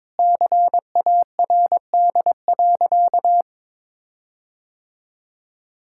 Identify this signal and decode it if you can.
Morse code, heard as a keyed tone, about 22 wpm, 700 Hz. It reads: CARD.
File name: Sentences with 7-Word Phrases - 22wpm